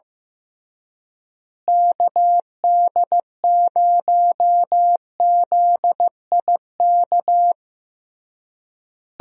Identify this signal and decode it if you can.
Morse code, heard as a keyed tone, about 15 wpm, 700 Hz. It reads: KD0ZIK
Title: USA Call Signs - 6 Characters - Repeated 3x - 15wpm